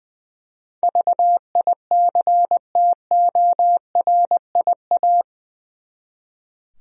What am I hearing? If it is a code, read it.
Morse code, heard as a keyed tone, about 20 wpm, 700 Hz. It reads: VICTORIA
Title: USA Common Names - 20wpm